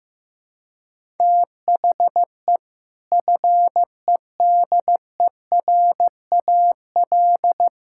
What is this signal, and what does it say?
Morse code, heard as a keyed tone, about 14 wpm, 700 Hz. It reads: THE FEDERAL
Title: Sentences from Top 1000 Words - Review Part 5 - 15wpm